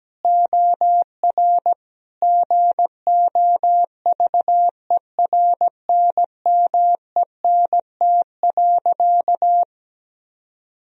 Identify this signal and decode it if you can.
Morse code, heard as a keyed tone, about 17 wpm, 700 Hz. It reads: OR GOVERNMENT.
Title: Sentences from Top 1000 Words - Review Part 3 - 17wpm